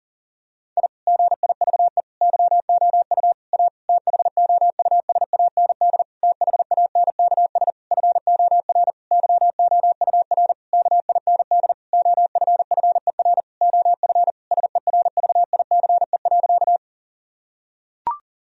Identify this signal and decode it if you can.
Morse code, heard as a keyed tone, about 40 wpm, 700 Hz. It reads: I GIVE YOU A THOUSAND THANKS FOR YOUR KIND OFFER OF SERVICE.
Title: Sentences from Top 1000 Words - Review Part 4 - 40wpm